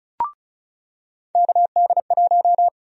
Morse code, heard as a keyed tone, about 36 wpm, 700 Hz. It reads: KD1
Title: USA Call Sign Prefixes - 35wpm